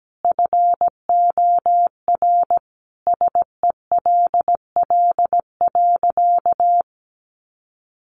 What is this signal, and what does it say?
Morse code, heard as a keyed tone, about 17 wpm, 700 Hz. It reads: FOR SELL.
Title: Sentences from Top 1300 Words 17wpm